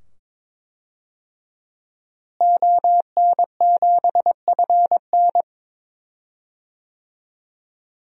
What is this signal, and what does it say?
Morse code, heard as a keyed tone, about 22 wpm, 700 Hz. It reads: ON7FN
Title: DXCC-100 Call Signs - 5 Characters - Repeated 2x - 22wpm